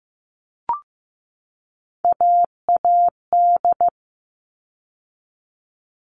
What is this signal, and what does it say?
Morse code, heard as a keyed tone, about 15 wpm, 700 Hz. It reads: AAD
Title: All Cut Numbers - 3-Digit 15wpm